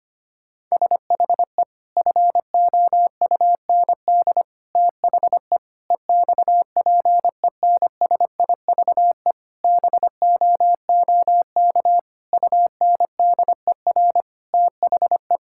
Morse code, heard as a keyed tone, about 25 wpm, 700 Hz. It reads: SHE FOUND THE EXPENSIVE BOOK UNDER THE
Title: Sentences from Top 1000 Words - Review Part 6 - 25wpm